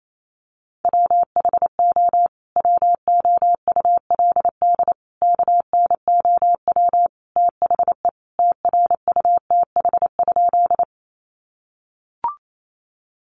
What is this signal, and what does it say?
Morse code, heard as a keyed tone, about 28 wpm, 700 Hz. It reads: WHO WOULD KNOW THE TRUTH?
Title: Sentences from Top 800 Words 28wpm